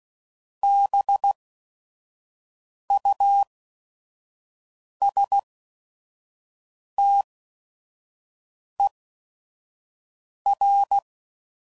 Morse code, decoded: BUSTER

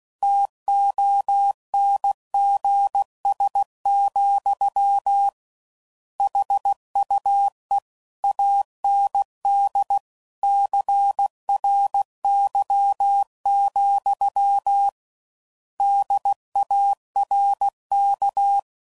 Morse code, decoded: TONGS, HUE AND CRY, DARK

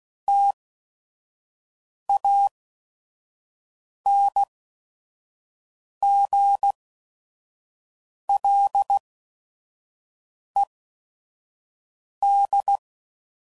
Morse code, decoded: TANGLED

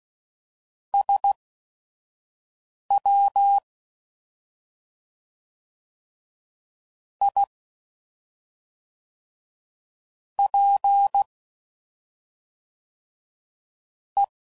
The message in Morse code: SWIPE

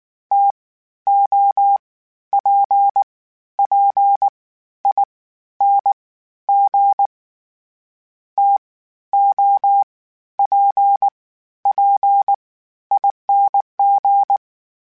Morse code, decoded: T O P P I N G T O P P I N G

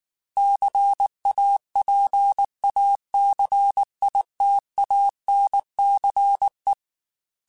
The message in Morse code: CAPACITANCE

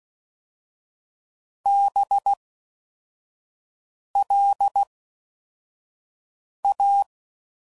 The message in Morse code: BLA

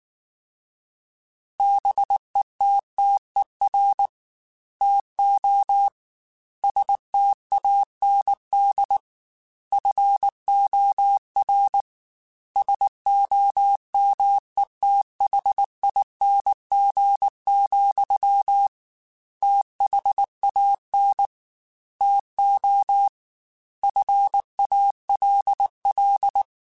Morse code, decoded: BETTER TO STAND FOR SOMETHING, THAN TO FALL

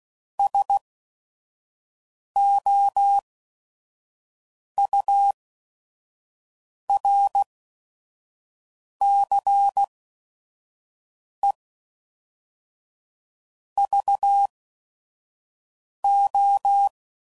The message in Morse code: S O U R C E V O

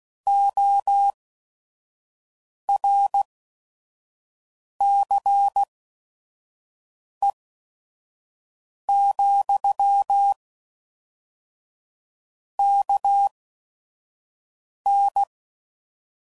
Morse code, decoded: O R C E , K N

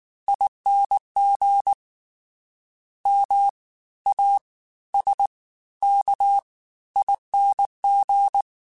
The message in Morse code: ING MASKING

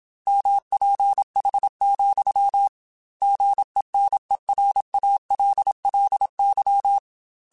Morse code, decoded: MPH, GENERALLY

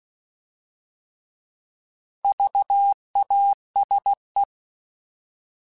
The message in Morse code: VASE

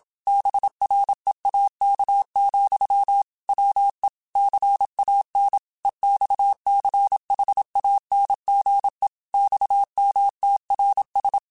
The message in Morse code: BREAK, WE CAN EXCHANGE XMTRS

